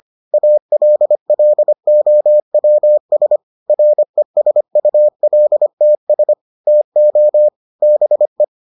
Morse code, heard as a keyed tone, about 25 wpm, 600 Hz. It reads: ALLOWS RESULTS TO BE